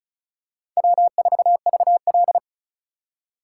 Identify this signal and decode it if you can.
Morse code, heard as a keyed tone, about 36 wpm, 700 Hz. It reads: W4VL
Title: USA Call Signs - 4 Characters - 3x Character Spacing - 35wpm